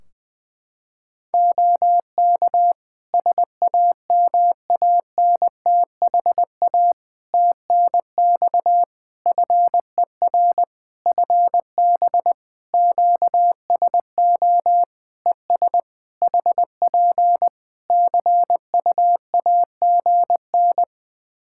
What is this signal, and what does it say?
Morse code, heard as a keyed tone, about 20 wpm, 700 Hz. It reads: OK SAMANTHA TNX FER FB QSO ES HP CUAGN
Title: Practice QSO - Exchange 4 - Ending - 20wpm